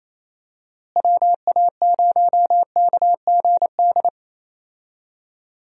Morse code, heard as a keyed tone, about 28 wpm, 700 Hz. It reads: WA0KGD
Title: USA Call Signs - 6 Characters - Repeated 2x - 28wpm